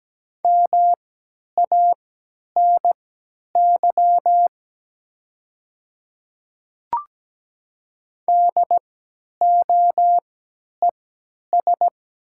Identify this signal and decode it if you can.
Morse code, heard as a keyed tone, about 17 wpm, spaced as for 10 wpm, 700 Hz. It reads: MANY DOES
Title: 3x Character Spacing - Top 200 Words - 17wpm